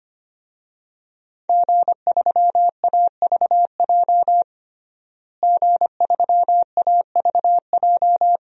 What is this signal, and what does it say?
Morse code, heard as a keyed tone, about 25 wpm, 700 Hz. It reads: G3AVJ G3AVJ
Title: DXCC-100 Call Signs - 5 Characters - Repeated 2x - 25wpm